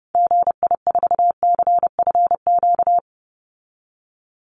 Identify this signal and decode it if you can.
Morse code, heard as a keyed tone, about 30 wpm, 700 Hz. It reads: GI4CFQ
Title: Call Signs Repeated 3x - 30wpm